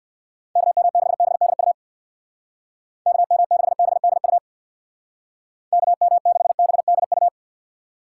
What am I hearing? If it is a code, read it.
Morse code, heard as a keyed tone, about 60 wpm, 690 Hz. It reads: KM6BDU KM6BDU KM6BDU
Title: Speed Racing - USA 2x3 Call Signs 50wpm